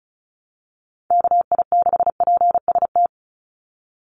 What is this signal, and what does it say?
Morse code, heard as a keyed tone, about 34 wpm, 700 Hz. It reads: KI6PST
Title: USA Call Signs 35wpm